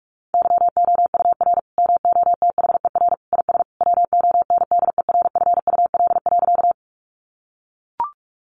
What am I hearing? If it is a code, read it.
Morse code, heard as a keyed tone, about 44 wpm, 700 Hz. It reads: YOUR MOTHER IS WONDERFUL.
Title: Sentences from Top 1100 Words 45wpm